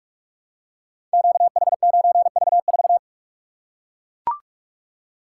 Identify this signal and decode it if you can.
Morse code, heard as a keyed tone, about 42 wpm, 700 Hz. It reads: QH9V4
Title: Code Groups - 5 Letter:Numbers - 45wpm